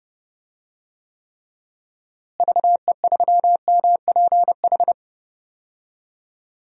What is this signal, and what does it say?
Morse code, heard as a keyed tone, about 30 wpm, 700 Hz. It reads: VE3MPH
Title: DXCC-100 Call Signs - 6 Characters - Repeated 3x - 30wpm